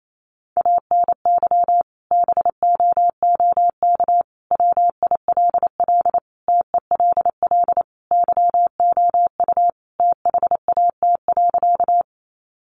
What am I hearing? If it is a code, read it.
Morse code, heard as a keyed tone, about 28 wpm, 700 Hz. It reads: ANY BOOK WILL TELL YOU THAT.
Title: Sentences from Top 300 Words 28wpm